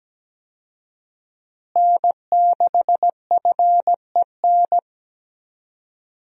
Morse code, heard as a keyed tone, about 17 wpm, 700 Hz. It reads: N6FEN
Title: USA Call Signs - 5 Characters - Repeated 3x - 17wpm